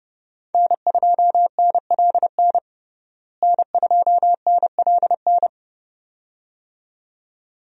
Morse code, heard as a keyed tone, about 31 wpm, 700 Hz. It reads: N2NLN N2NLN
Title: USA Call Signs - 5 Characters - Repeated 2x - 30wpm